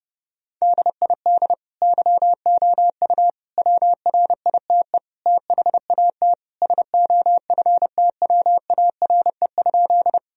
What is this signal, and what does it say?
Morse code, heard as a keyed tone, about 30 wpm, 700 Hz. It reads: DID YOU WRITE THAT SOFTWARE?
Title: Sentences from Top 1000 Words - Review Part 3 - 30wpm